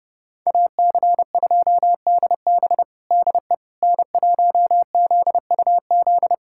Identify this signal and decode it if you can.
Morse code, heard as a keyed tone, about 30 wpm, 700 Hz. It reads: AC2DB DE N1ZUZ
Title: USA Callsign DE Callsign - 5 Characters - 30wpm